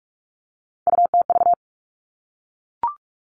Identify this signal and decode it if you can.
Morse code, heard as a keyed tone, about 44 wpm, 700 Hz. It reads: UTV
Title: All Cut Numbers - 3-Digit 45wpm